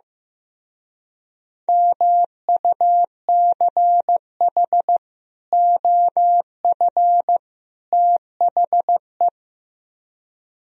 Morse code, heard as a keyed tone, about 15 wpm, 700 Hz. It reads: MUCH OF THE
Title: Speed-Racing - Sets of 3 Words 15wpm